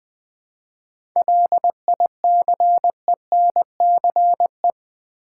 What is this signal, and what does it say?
Morse code, heard as a keyed tone, about 20 wpm, 700 Hz. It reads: LICENCE
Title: 7 Letter Words - First 3 Letters Given - 20wpm